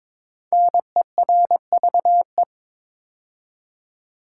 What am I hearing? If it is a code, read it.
Morse code, heard as a keyed tone, about 23 wpm, 690 Hz. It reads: NERVE
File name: 5 Letter Words - First 3 Letters Given - 22wpm